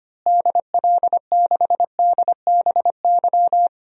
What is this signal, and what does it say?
Morse code, heard as a keyed tone, about 25 wpm, 700 Hz. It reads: DL6DBY
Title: DXCC-100 Call Signs - 6 Characters - Repeated 3x - 25wpm